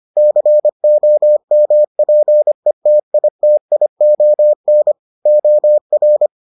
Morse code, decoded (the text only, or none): COMPETITION OR